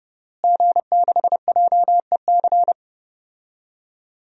Morse code, heard as a keyed tone, about 29 wpm, 700 Hz. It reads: G6JEC
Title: DXCC-100 Call Signs - 5 Characters - Repeated 2x - 30wpm